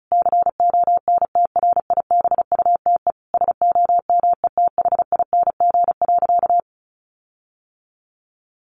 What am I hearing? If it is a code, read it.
Morse code, heard as a keyed tone, about 34 wpm, 700 Hz. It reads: CONTRIBUTE SOMETHING.
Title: Sentences from Top 1400 Words 35wpm